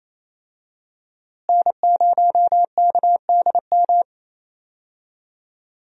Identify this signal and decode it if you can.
Morse code, heard as a keyed tone, about 28 wpm, 700 Hz. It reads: N0KDM
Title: USA Call Signs - 5 Characters - Repeated 2x - 28wpm